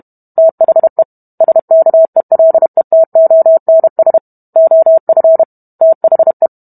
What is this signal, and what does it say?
Morse code, heard as a keyed tone, about 32 wpm, 650 Hz. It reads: THE SKELETONS OF THE